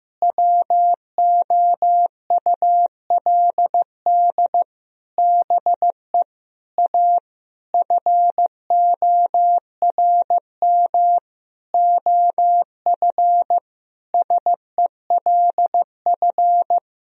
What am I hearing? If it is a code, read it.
Morse code, heard as a keyed tone, about 15 wpm, 700 Hz. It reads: WOULD BE A FORM OF SELF
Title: Sentences from Top 1000 Words - Review Part 1 - 15wpm